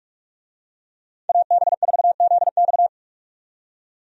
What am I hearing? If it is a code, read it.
Morse code, heard as a keyed tone, about 43 wpm, 700 Hz. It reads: AB4ZX
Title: USA Call Signs - 5 Characters - 45wpm